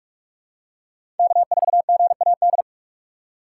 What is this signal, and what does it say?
Morse code, heard as a keyed tone, about 46 wpm, 700 Hz. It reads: K4GAD
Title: USA Call Signs - 5 Characters - 3x Character Spacing - 45wpm